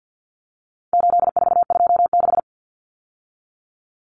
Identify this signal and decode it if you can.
Morse code, heard as a keyed tone, about 50 wpm, 700 Hz. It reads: 8426
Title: Numbers - 4-Digit 50wpm